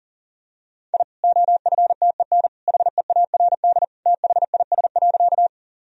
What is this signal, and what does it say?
Morse code, heard as a keyed tone, about 40 wpm, 700 Hz. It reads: I OFTEN HEARD THIS.
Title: Sentences from Top 1000 Words - Review Part 1 - 40wpm